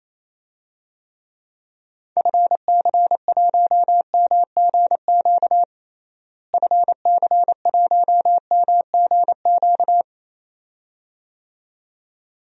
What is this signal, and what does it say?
Morse code, heard as a keyed tone, about 28 wpm, 700 Hz. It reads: FC1MGQ FC1MGQ
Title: DXCC-100 Call Signs - 6 Characters - Repeated 2x - 28wpm